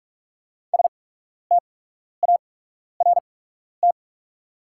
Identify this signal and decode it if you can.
Morse code, heard as a keyed tone, about 44 wpm, spaced as for 12 wpm, 690 Hz. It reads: START
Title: 8x Character Spacing - Top 500 Words - 45wpm